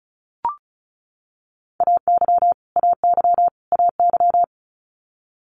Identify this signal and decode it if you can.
Morse code, heard as a keyed tone, about 35 wpm, 700 Hz. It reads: AY AY AY
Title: Multi-letter Phonemes Repeated 3x 35wpm